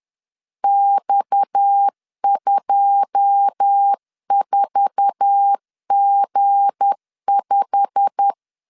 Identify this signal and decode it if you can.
Morse code, heard as a keyed tone, about 10 wpm, 800 Hz. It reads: X24G5
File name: CW / DET / 1 9 1500C S A 10wpm 4511